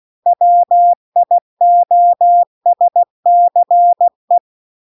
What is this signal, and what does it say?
Morse code, heard as a keyed tone, about 16 wpm, 700 Hz. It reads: WIOSCE